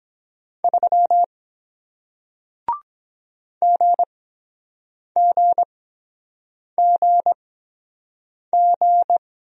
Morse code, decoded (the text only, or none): 3 GGGG